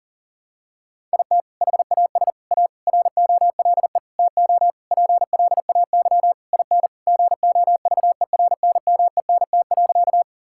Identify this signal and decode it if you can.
Morse code, heard as a keyed tone, about 37 wpm, 700 Hz. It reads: IT HAS A ROLE TO PLAY IN GOVERNMENT.
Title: Sentences from Top 1000 Words - Review Part 6 - 40wpm